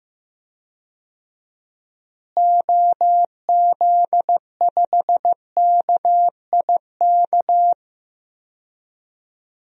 Morse code, heard as a keyed tone, about 15 wpm, 700 Hz. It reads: OZ5KIK
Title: DXCC-100 Call Signs - 6 Characters - Repeated 2x - 15wpm